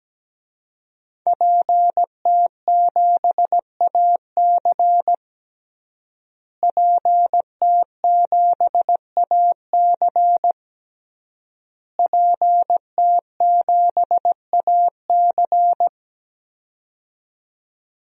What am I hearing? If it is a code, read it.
Morse code, heard as a keyed tone, about 17 wpm, 700 Hz. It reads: PT7AC PT7AC PT7AC
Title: DXCC-100 Call Signs - 5 Characters - Repeated 3x - 17wpm